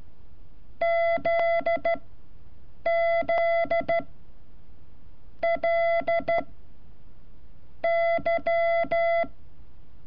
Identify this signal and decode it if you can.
Morse code, heard as a keyed tone, about 13 wpm, spaced as for 6 wpm, 680 Hz. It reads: ZZLY
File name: curso / ejercicio 31